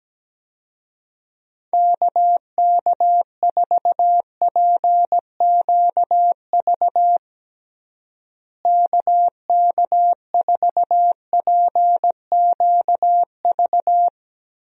Morse code, heard as a keyed tone, about 17 wpm, 700 Hz. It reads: KK4PQV KK4PQV
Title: USA Call Signs - 6 Characters - Repeated 2x - 17wpm